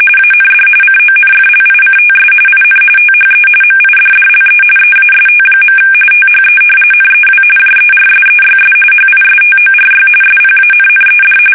RTTY, about 45.45 baud. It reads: AND CONFLICTS IN THE MIDDLE EAST. THE NEW JERSEYS HISTORY SPANNED
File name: fsk850 / battleshipnj